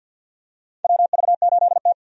Morse code, heard as a keyed tone, about 49 wpm, 700 Hz. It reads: WV8T
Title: USA Call Signs - 4 Characters - 50wpm